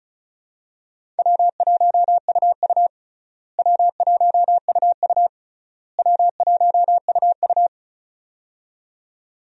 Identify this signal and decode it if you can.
Morse code, heard as a keyed tone, about 36 wpm, 700 Hz. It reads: W1UU W1UU W1UU
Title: USA Call Signs - 4 Characters - Repeated 3x - 35wpm